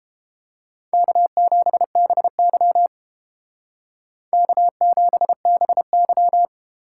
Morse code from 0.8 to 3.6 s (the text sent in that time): K7BY